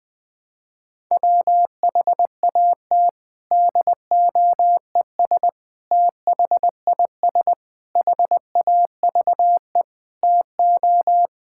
Morse code, heard as a keyed tone, about 21 wpm, 700 Hz. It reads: WHAT DOES THIS HAVE TO
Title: Sentences with 7-Word Phrases - Spoken Beginning - 20wpm